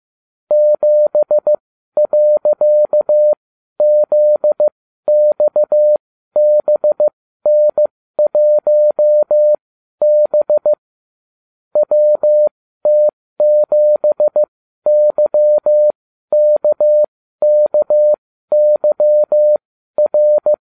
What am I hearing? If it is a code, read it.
Morse code, heard as a keyed tone, about 15 wpm, spaced as for 12 wpm, 600 Hz. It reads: 7.ZXBN1B WT7YKKYR